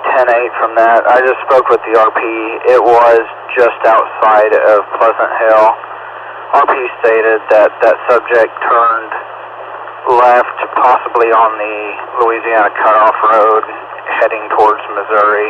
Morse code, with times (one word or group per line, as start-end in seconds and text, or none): none